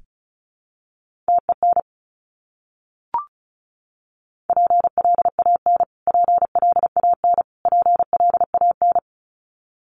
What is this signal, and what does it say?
Morse code, heard as a keyed tone, about 34 wpm, 700 Hz. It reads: TEN PLAN PLAN PLAN
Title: Top 500 Words Repeated 35wpm